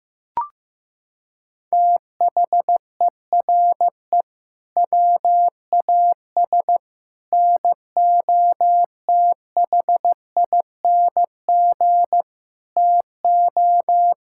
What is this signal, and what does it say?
Morse code, heard as a keyed tone, about 15 wpm, 700 Hz. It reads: THERE WAS NOTHING TO
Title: Sentences from Top 1000 Words - Review Part 6 - 15wpm